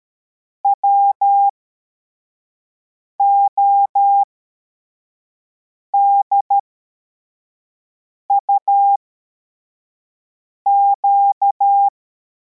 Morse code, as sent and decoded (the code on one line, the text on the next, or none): .-- --- -.. ..- --.-
WODUQ